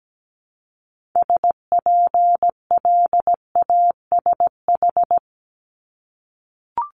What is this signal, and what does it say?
Morse code, heard as a keyed tone, about 17 wpm, 700 Hz. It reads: SPLASH
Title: Multi-letter Phonemes Repeated 3x 17wpm